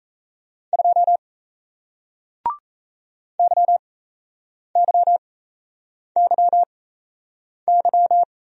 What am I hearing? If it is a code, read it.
Morse code, heard as a keyed tone, about 38 wpm, spaced as for 8 wpm, 700 Hz. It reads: 2 YYYY